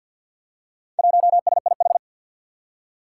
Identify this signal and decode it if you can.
Morse code, heard as a keyed tone, about 52 wpm, 700 Hz. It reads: 1SIH
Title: Four Challenging Letter-Number Combinations 50wpm